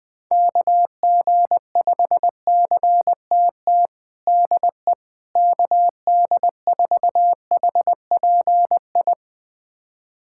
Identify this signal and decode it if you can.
Morse code, heard as a keyed tone, about 20 wpm, 700 Hz. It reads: KG5CTT DE KD4HPI
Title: USA Callsign DE Callsign 20wpm